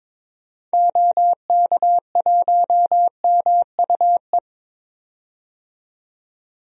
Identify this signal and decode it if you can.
Morse code, heard as a keyed tone, about 22 wpm, 700 Hz. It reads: OK1MUE
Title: DXCC-100 Call Signs - 6 Characters - 22wpm